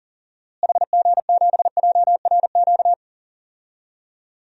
Morse code, heard as a keyed tone, about 40 wpm, 700 Hz. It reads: HG7JRQ
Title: DXCC-100 Call Signs - 6 Characters - Repeated 2x - 40wpm